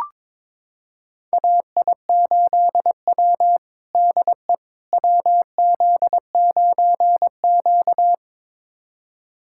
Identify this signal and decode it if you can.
Morse code, heard as a keyed tone, about 22 wpm, 700 Hz. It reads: AI8W DE WZ9Q
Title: USA Callsign DE Callsign - 4 Characters - 22wpm